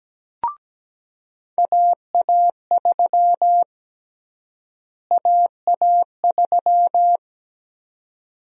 Morse code, heard as a keyed tone, about 17 wpm, 700 Hz. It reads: AA3 AA3
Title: USA Call Sign Prefixes - Repeated 2x - 17wpm